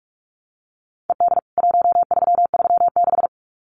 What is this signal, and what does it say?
Morse code, heard as a keyed tone, about 44 wpm, 700 Hz. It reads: ED 1336
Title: CWT Exchange 45wpm